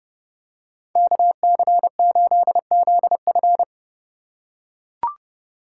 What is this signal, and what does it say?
Morse code, heard as a keyed tone, about 30 wpm, 700 Hz. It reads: KC8ZF